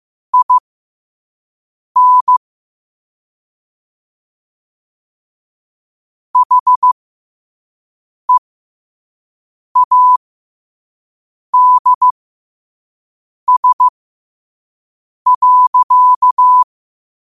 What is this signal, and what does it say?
Morse code, heard as a keyed tone, about 15 wpm, spaced as for 5 wpm, 1000 Hz. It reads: IN HEADS.
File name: qotdfiles / 5wpm